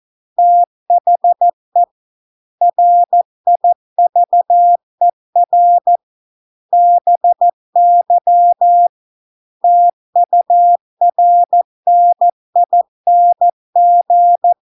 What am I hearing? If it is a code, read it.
Morse code, heard as a keyed tone, about 14 wpm, 700 Hz. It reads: THE RIVER BY TURNING